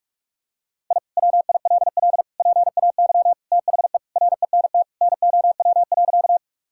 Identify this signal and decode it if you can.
Morse code, heard as a keyed tone, about 45 wpm, 700 Hz. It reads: I WILL PAY THE RENT NOW.
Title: Sentences from Top 1000 Words - Review Part 5 - 45wpm